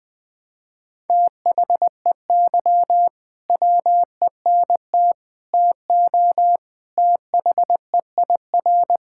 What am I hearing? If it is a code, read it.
Morse code, heard as a keyed tone, about 20 wpm, 700 Hz. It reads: THEY WENT TO THEIR
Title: Sentences with 2-Word Phrases - Spoken Beginning - 20wpm